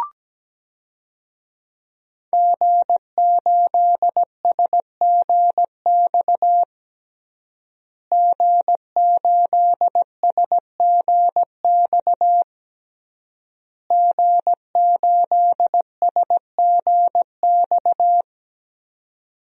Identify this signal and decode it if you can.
Morse code, heard as a keyed tone, about 17 wpm, 700 Hz. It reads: G8SGX G8SGX G8SGX
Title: DXCC-100 Call Signs - 5 Characters - Repeated 3x - 17wpm